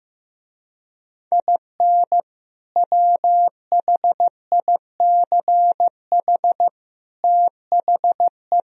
Morse code, heard as a keyed tone, about 15 wpm, 700 Hz. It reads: IN WHICH THE
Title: Speed-Racing - Sets of 3 Words 15wpm